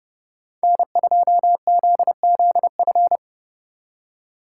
Morse code, heard as a keyed tone, about 36 wpm, 700 Hz. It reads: N2ZZF
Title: Speed Racing - USA 1x3 Call Signs 30wpm